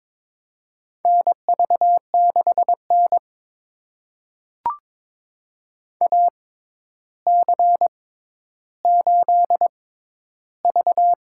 Morse code, heard as a keyed tone, about 22 wpm, spaced as for 11 wpm, 700 Hz. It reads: NV6N AC8V
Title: USA Call Signs - 4 Characters - 6x Character Spacing - 22wpm